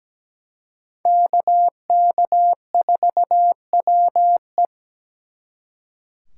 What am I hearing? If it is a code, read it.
Morse code, heard as a keyed tone, about 20 wpm, 700 Hz. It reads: KK4WE
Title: Speed Racing - USA 2x2 Call Signs 17wpm